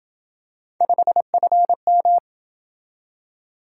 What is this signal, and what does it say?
Morse code, heard as a keyed tone, about 27 wpm, 700 Hz. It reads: 5FM